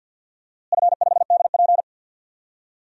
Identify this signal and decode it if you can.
Morse code, heard as a keyed tone, about 46 wpm, 700 Hz. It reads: F5DP
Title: DXCC-100 Call Signs - 4 Characters - Repeated 2x - 50wpm